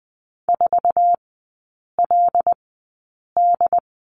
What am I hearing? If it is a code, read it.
Morse code, heard as a keyed tone, about 20 wpm, spaced as for 8 wpm, 700 Hz. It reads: 4LD
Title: 074 Lesson 13e - Letter L - Three Characters 20wpm